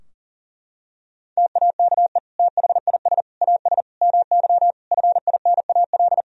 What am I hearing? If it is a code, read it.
Morse code, heard as a keyed tone, about 37 wpm, 700 Hz. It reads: TAKE THIS AS MY FINAL
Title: Sentences from Top 1000 Words - Review Part 2 - 40wpm